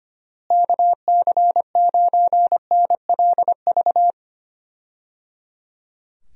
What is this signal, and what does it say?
Morse code, heard as a keyed tone, about 25 wpm, 700 Hz. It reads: KC9NLV
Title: USA Call Signs - 6 Characters - Repeated 3x - 25wpm